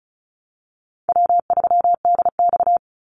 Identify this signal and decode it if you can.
Morse code, heard as a keyed tone, about 34 wpm, 700 Hz. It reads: W3DX
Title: NA SOTA Activator Callsigns 35wpm